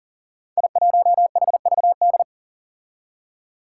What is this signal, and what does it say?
Morse code, heard as a keyed tone, about 40 wpm, 700 Hz. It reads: I1HVD
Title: Five Challenging Letter-Number Combinations 40wpm